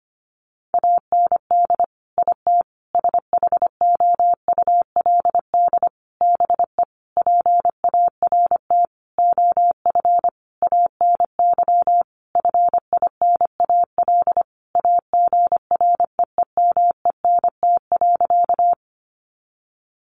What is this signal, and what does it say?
Morse code, heard as a keyed tone, about 25 wpm, 700 Hz. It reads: AND IT SHOULD BE PART OF ANY FINAL AGREEMENT.
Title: Sentences from Top 1500 Words 25wpm